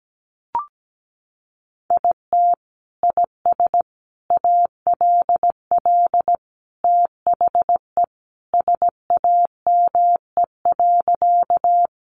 Morse code, heard as a keyed tone, about 17 wpm, 700 Hz. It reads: IT IS ALL THE SAME.
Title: Sentences from Top 200 Words 17wpm